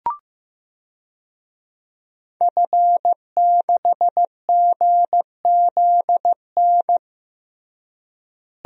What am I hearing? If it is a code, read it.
Morse code, heard as a keyed tone, about 15 wpm, 700 Hz. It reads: F6GZN
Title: DXCC-100 Call Signs - 5 Characters - Repeated 3x - 15wpm